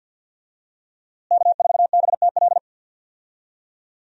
Morse code, heard as a keyed tone, about 52 wpm, 700 Hz. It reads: K4BTL
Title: USA Call Signs - 5 Characters - Repeated 3x - 50wpm